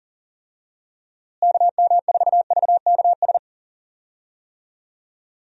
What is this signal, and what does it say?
Morse code, heard as a keyed tone, about 40 wpm, 700 Hz. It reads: KM4VKS
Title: USA Call Signs - 6 Characters - Repeated 2x - 40wpm